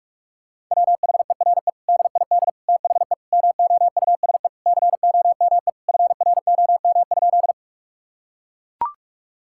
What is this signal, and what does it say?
Morse code, heard as a keyed tone, about 41 wpm, 700 Hz. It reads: WHERE DID THE MOUSE COME FROM?
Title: Sentences from Top 1000 Words - Review Part 5 - 45wpm